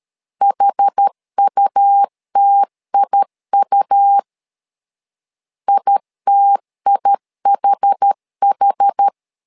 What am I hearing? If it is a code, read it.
Morse code, heard as a keyed tone, about 13 wpm, 800 Hz. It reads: HUTIU ITIHH
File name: LEZIONI / 2 UHDN / 2 UH 12wpm 1439